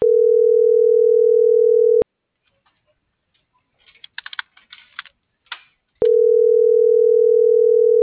Ringback tone (North America), 440 + 480 Hz.
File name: CENTRAL TERRESTRE / out karis - .29998